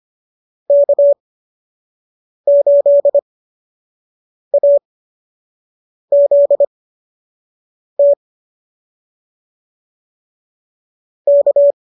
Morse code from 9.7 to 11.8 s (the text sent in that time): K